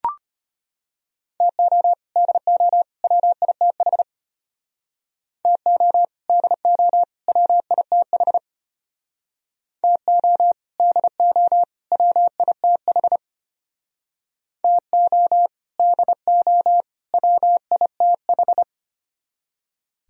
Morse code, heard as a keyed tone, about 33 wpm, 700 Hz. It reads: TO DO WITH TO DO WITH TO DO WITH TO DO WITH